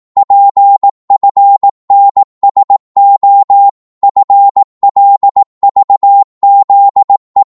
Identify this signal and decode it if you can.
Morse code, heard as a keyed tone, about 18 wpm, 800 Hz. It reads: PFNSO FLVZE
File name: Morse / ltr-18wpm